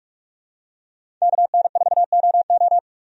Morse code, heard as a keyed tone, about 41 wpm, 700 Hz. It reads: KN4OO
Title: USA Call Signs - 5 Characters - Repeated 3x - 45wpm